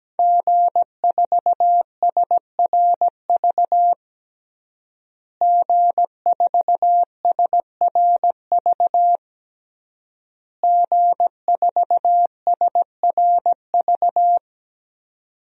Morse code, heard as a keyed tone, about 17 wpm, 700 Hz. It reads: G4SRV G4SRV G4SRV
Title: DXCC-100 Call Signs - 5 Characters - Repeated 3x - 17wpm